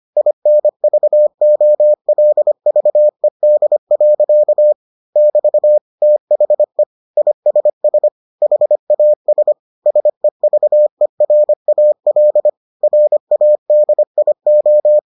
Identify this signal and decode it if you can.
Morse code, heard as a keyed tone, about 25 wpm, 600 Hz. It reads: INVOLVED. <BT> THE ISS HAS SEVERAL RADIO